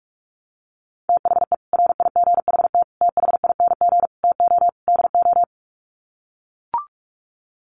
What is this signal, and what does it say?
Morse code, heard as a keyed tone, about 44 wpm, 700 Hz. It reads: THE RIGHT THING TO DO
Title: Sets of 5 Words 45wpm